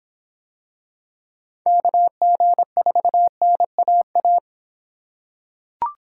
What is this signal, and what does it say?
Morse code, heard as a keyed tone, about 26 wpm, 700 Hz. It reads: KG4NAA